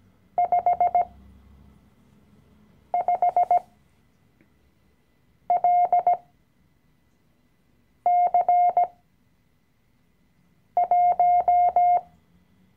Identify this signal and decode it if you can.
Morse code, heard as a keyed tone, about 17 wpm, spaced as for 4 wpm, 710 Hz. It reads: 55LC1
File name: CW-Online-Kurs / Anfänger / Lektion 012